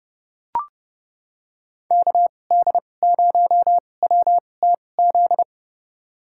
Morse code, heard as a keyed tone, about 30 wpm, spaced as for 22 wpm, 700 Hz. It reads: KD0WTZ
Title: USA Call Signs - 6 Characters - 2x Character Spacing - 30wpm